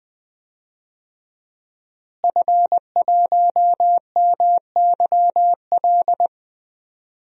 Morse code, heard as a keyed tone, about 20 wpm, 700 Hz. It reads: F1MYL
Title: DXCC-100 Call Signs - 5 Characters - Repeated 3x - 20wpm